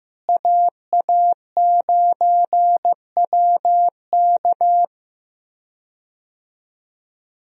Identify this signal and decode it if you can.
Morse code, heard as a keyed tone, about 16 wpm, 700 Hz. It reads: AA9WK
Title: USA Call Signs - 5 Characters - Repeated 3x - 15wpm